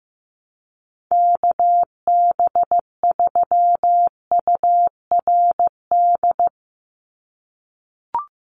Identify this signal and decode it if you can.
Morse code, heard as a keyed tone, about 15 wpm, 700 Hz. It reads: KB3URD
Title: USA Call Signs 15wpm